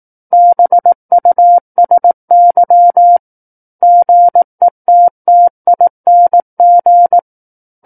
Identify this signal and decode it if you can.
Morse code, heard as a keyed tone, about 18 wpm, 700 Hz. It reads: BUSY GETTING
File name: Book One / 09-Chapter 9 - 18 WPM